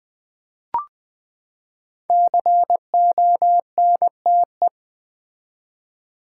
Morse code, heard as a keyed tone, about 20 wpm, 700 Hz. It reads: CONTE
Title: Build A Word - 5 Letters At A Time - 12-Letter Words - 20wpm